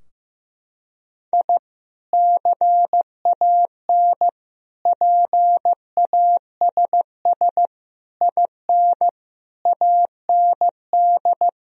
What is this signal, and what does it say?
Morse code, heard as a keyed tone, about 15 wpm, 700 Hz. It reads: I CAN PASS IN AND
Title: Sentences from Top 1000 Words - Review Part 4 - 15wpm